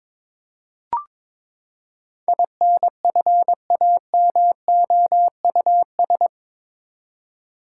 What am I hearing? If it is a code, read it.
Morse code, heard as a keyed tone, about 22 wpm, 700 Hz. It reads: INFAMOUS
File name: practice / 22wpm / Eight Letter Words - 22wpm